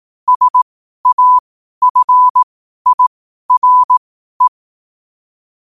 Morse code, heard as a keyed tone, about 18 wpm, spaced as for 12 wpm, 1000 Hz. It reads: SAFIRE
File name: qotdfiles / 12wpm